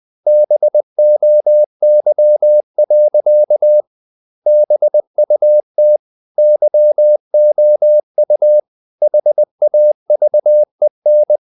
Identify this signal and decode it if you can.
Morse code, heard as a keyed tone, about 20 wpm, 600 Hz. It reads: BOY. BUT YOU HAVEN
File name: the old man and the sea / the old man and the sea part 1 20wpm